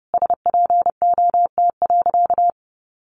Morse code, heard as a keyed tone, about 30 wpm, 700 Hz. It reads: SPOT.
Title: Sentences from Top 1100 Words 30wpm